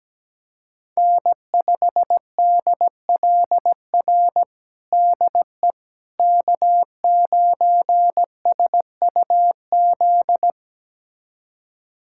Morse code, heard as a keyed tone, about 17 wpm, 700 Hz. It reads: N5DLR DE K9SUZ
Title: USA Callsign DE Callsign - 5 Characters - 17wpm